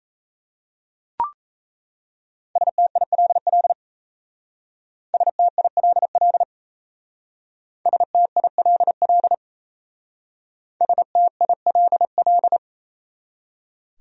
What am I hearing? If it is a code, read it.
Morse code, heard as a keyed tone, about 37 wpm, 700 Hz. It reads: STILL STILL STILL STILL